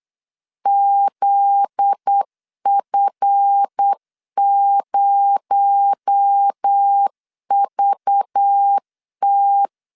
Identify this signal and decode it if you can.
Morse code, heard as a keyed tone, about 8 wpm, 800 Hz. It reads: ZF0VT
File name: LEZIONI / 7 GF9Z / 7 GF9Z 8wpm 3213